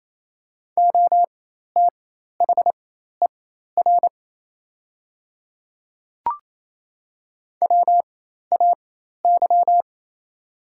Morse code, heard as a keyed tone, about 28 wpm, spaced as for 13 wpm, 700 Hz. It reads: OTHER WAY
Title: 4x Character Spacing - Top 100 Words - 28wpm